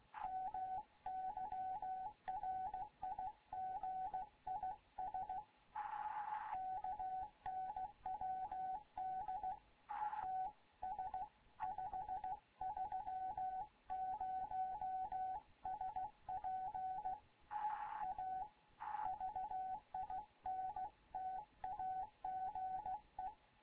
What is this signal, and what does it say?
Morse code, heard as a keyed tone, about 16 wpm, 750 Hz, with radio noise. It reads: MY RIG IS KNWD TS530SP A VINTAGE
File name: Morze-HA2MN / hangp / 2 11 GYAKORLAT